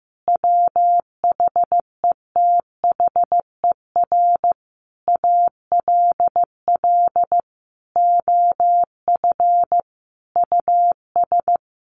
Morse code, heard as a keyed tone, about 15 wpm, 700 Hz. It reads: WHETHER ALL OF US
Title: A Sound Of Thunder 15wpm